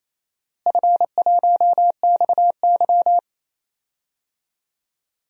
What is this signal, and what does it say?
Morse code, heard as a keyed tone, about 28 wpm, 700 Hz. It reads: F1XY
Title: DXCC-100 Call Signs - 4 Characters - Repeated 2x - 28wpm